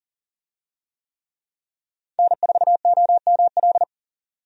Morse code, heard as a keyed tone, about 38 wpm, 700 Hz. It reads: N4OML
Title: USA Call Signs - 5 Characters - 40wpm